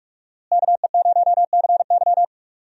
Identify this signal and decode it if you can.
Morse code, heard as a keyed tone, about 46 wpm, 700 Hz. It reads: KE0CY
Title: USA Call Signs - 5 Characters - Repeated 2x - 45wpm